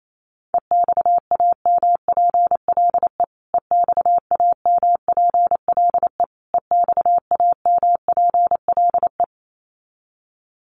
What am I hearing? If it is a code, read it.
Morse code, heard as a keyed tone, about 28 wpm, 700 Hz. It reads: EXAMPLE EXAMPLE EXAMPLE
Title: Top 500 Words Repeated 28wpm